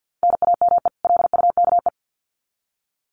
Morse code, heard as a keyed tone, about 50 wpm, 700 Hz. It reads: NAME LUKE
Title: Common QSO Phrases 50wpm